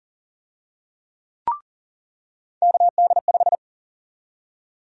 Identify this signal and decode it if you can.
Morse code, heard as a keyed tone, about 40 wpm, 700 Hz. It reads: KD5
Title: USA Call Sign Prefixes - 40wpm